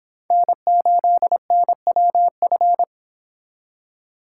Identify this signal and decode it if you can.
Morse code, heard as a keyed tone, about 26 wpm, 700 Hz. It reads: N8NWF